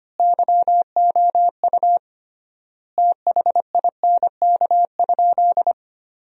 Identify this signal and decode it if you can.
Morse code, heard as a keyed tone, about 25 wpm, 700 Hz. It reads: YOU THINK?
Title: 3x Word-Spacing - Sentences from Top 200 Words - 25wpm